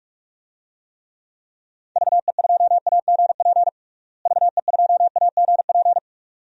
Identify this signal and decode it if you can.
Morse code, heard as a keyed tone, about 46 wpm, 700 Hz. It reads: VE2AGP VE2AGP
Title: DXCC-100 Call Signs - 6 Characters - Repeated 2x - 45wpm